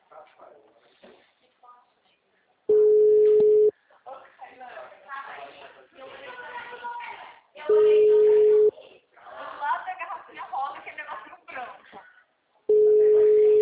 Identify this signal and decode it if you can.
Ringback tone (Europe, 425 Hz), with speech.